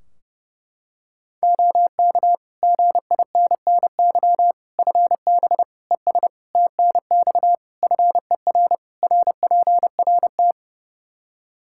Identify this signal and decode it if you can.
Morse code, heard as a keyed tone, about 30 wpm, 700 Hz. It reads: OK GINNY FB ES TNX FER RPRT
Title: Practice QSO - Exchange 2 - Info - 30wpm